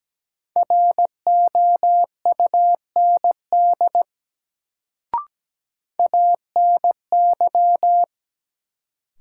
Morse code, heard as a keyed tone, about 17 wpm, 700 Hz. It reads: ROUND ANY
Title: Sentences from Top 200 Words - New Word Drill - 17wpm